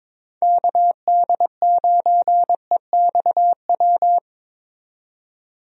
Morse code, heard as a keyed tone, about 22 wpm, 700 Hz. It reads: KD9EXW
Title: USA Call Signs - 6 Characters - 22wpm